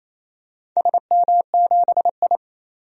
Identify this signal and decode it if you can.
Morse code, heard as a keyed tone, about 27 wpm, 700 Hz. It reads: SM7I
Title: DXCC-100 Call Signs - 4 Characters - 28wpm